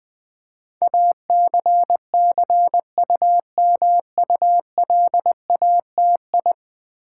Morse code, heard as a keyed tone, about 20 wpm, 700 Hz. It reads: ACCUMULATI
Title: Build A Word - 5 Letters At A Time - 12-Letter Words - 20wpm